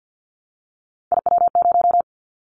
Morse code, heard as a keyed tone, about 50 wpm, 700 Hz. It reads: IW0
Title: Callsign Prefixes for Top 50 DXCC - 50wpm